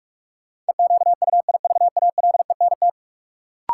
Morse code, heard as a keyed tone, about 42 wpm, 700 Hz. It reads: EQUIVALENT
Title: Sentences from Top 1600 Words - New Word Drill Alt - 45wpm